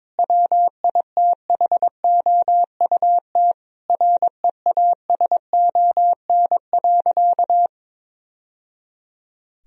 Morse code, heard as a keyed tone, about 21 wpm, 700 Hz. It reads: WITHOUT REASON.
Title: Sentences from Top 1000 Words - Review Part 3 - 22wpm